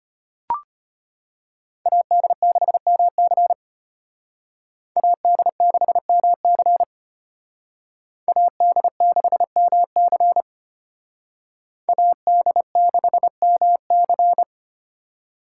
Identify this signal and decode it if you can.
Morse code, heard as a keyed tone, about 32 wpm, 700 Hz. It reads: AD6MC AD6MC AD6MC AD6MC